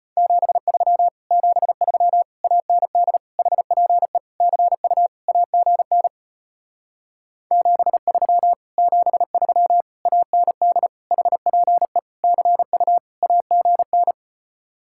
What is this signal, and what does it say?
Morse code, heard as a keyed tone, about 33 wpm, 700 Hz. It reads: 73 73 AND HPE CU AGN 73 73 AND HPE CU AGN